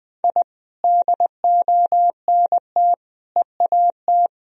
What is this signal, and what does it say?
Morse code, heard as a keyed tone, about 20 wpm, 700 Hz. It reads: I DONT EAT
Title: Sentences from Top 1000 Words - Review Part 3 - 20wpm